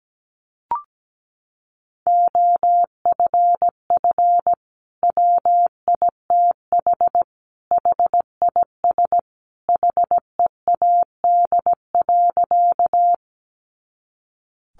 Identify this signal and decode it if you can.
Morse code, heard as a keyed tone, about 17 wpm, 700 Hz. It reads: OFF WITH HIS HEAD.
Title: Sentences from Top 300 Words 17wpm